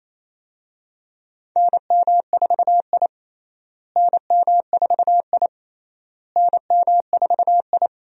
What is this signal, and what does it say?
Morse code, heard as a keyed tone, about 28 wpm, 700 Hz. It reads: NM4I NM4I NM4I
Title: USA Call Signs - 4 Characters - Repeated 3x - 28wpm